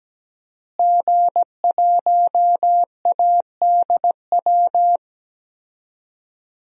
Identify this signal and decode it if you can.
Morse code, heard as a keyed tone, about 17 wpm, 700 Hz. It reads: G1ADW
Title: DXCC-100 Call Signs - 5 Characters - Repeated 2x - 17wpm